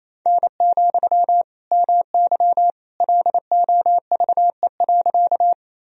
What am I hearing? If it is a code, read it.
Morse code, heard as a keyed tone, about 27 wpm, 700 Hz. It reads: N, MY LOVE.
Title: Sentences from Top 400 Words and Abbreviations 28wpm